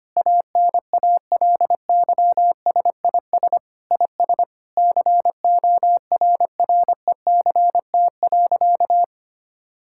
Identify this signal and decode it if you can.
Morse code, heard as a keyed tone, about 25 wpm, 700 Hz. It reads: ANALYSIS IS CORRECT.
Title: Sentences from Top 1000 Words - Review Part 5 - 25wpm